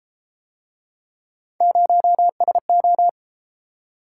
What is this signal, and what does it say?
Morse code, heard as a keyed tone, about 31 wpm, 700 Hz. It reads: 0SO